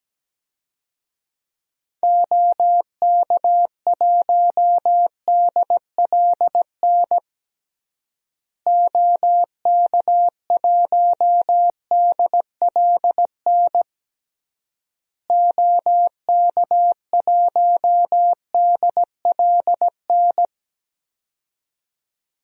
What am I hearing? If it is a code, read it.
Morse code, heard as a keyed tone, about 17 wpm, 700 Hz. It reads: OK1DLN OK1DLN OK1DLN
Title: DXCC-100 Call Signs - 6 Characters - Repeated 3x - 17wpm